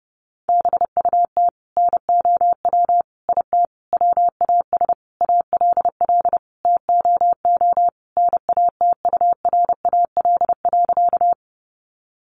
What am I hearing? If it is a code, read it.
Morse code, heard as a keyed tone, about 30 wpm, 700 Hz. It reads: BUT NOW IT WAS ALL TOO NATURAL.
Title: Sentences from Top 700 Words 30wpm